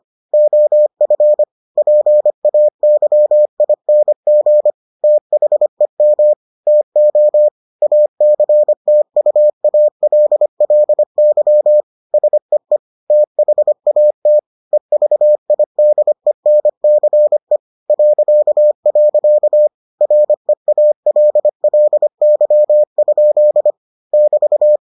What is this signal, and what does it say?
Morse code, heard as a keyed tone, about 25 wpm, 600 Hz. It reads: OF PAYING THEM TO ACTUALLY SEE THAT EVIDENCE .. REALLY? <BT>